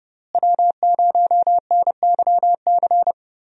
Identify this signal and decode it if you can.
Morse code, heard as a keyed tone, about 30 wpm, 700 Hz. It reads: W0NYC
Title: USA Call Signs - 5 Characters - Repeated 2x - 30wpm